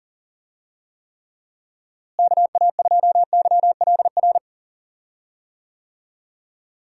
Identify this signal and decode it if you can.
Morse code, heard as a keyed tone, about 40 wpm, 700 Hz. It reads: KA2YLR
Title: USA Call Signs - 6 Characters - Repeated 3x - 40wpm